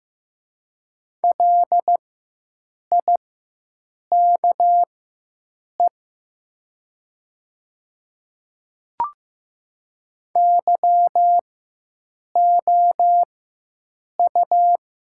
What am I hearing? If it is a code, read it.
Morse code, heard as a keyed tone, about 15 wpm, spaced as for 7 wpm, 700 Hz. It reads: LIKE YOU
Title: 4x Character Spacing - Top 100 Words - 15wpm